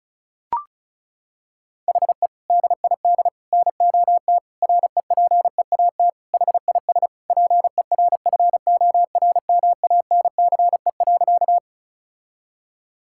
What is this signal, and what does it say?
Morse code, heard as a keyed tone, about 36 wpm, 700 Hz. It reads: HE DID NOT REPEAT HIS PERFORMANCE.
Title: Sentences from Top 1600 Words 35wpm